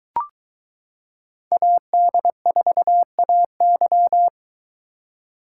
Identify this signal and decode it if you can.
Morse code, heard as a keyed tone, about 22 wpm, 700 Hz. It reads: AD4AY